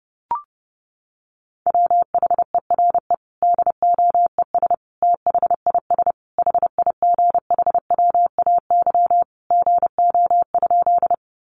Morse code, heard as a keyed tone, about 30 wpm, 700 Hz. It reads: WHERE DOES THIS HIGHWAY GO?
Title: Sentences from Top 1400 Words 30wpm